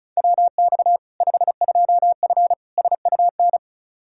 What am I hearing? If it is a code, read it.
Morse code, heard as a keyed tone, about 36 wpm, 700 Hz. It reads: WX 52F SUN
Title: Practice QSO - Exchange 2 - Info - 35wpm